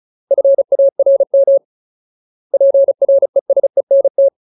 Morse code, heard as a keyed tone, about 34 wpm, 550 Hz. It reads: FARM PRESENT
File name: Ultimate Word List / 35WPM Ultimate